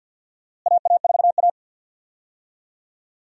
Morse code, heard as a keyed tone, about 50 wpm, 700 Hz. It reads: AA4A
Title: USA Call Signs - 4 Characters - Repeated 2x - 50wpm